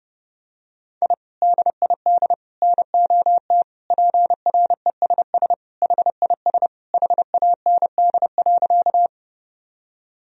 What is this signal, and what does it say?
Morse code, heard as a keyed tone, about 32 wpm, 700 Hz. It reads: I DID NOT PRESS HIS HAND.
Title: Sentences from Top 1000 Words - Review Part 1 - 30wpm